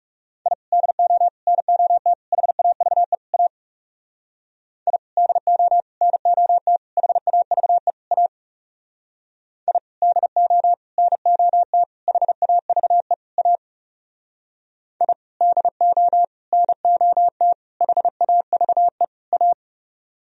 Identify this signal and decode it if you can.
Morse code, heard as a keyed tone, about 36 wpm, 700 Hz. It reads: I DO NOT HAVE A I DO NOT HAVE A I DO NOT HAVE A I DO NOT HAVE A
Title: Speed-Racing - Sets of 5 Words 30wpm